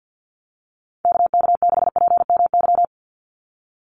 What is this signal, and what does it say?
Morse code, heard as a keyed tone, about 50 wpm, 700 Hz. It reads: KK6PMY
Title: USA Call Signs 50wpm